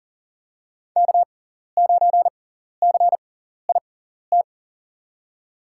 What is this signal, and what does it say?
Morse code, heard as a keyed tone, about 37 wpm, spaced as for 13 wpm, 700 Hz. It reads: K9CIT
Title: USA Call Signs - 6x Character Spacing - 40wpm